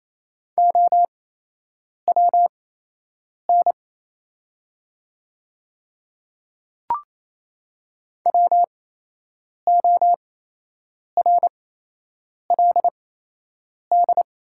Morse code, heard as a keyed tone, about 28 wpm, spaced as for 8 wpm, 700 Hz. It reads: OWN WORLD
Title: 8x Character Spacing - Top 300 Words - 28wpm